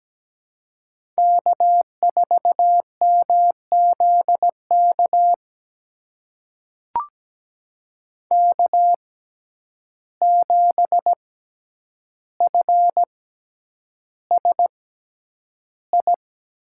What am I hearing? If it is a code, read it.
Morse code, heard as a keyed tone, about 17 wpm, spaced as for 9 wpm, 700 Hz. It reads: K4MZK K7FSI
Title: USA Call Signs - 5 Characters - 6x Character Spacing - 17wpm